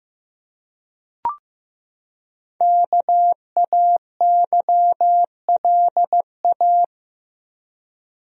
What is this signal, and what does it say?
Morse code, heard as a keyed tone, about 15 wpm, 700 Hz. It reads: KAYLA
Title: USA Common Names - 15wpm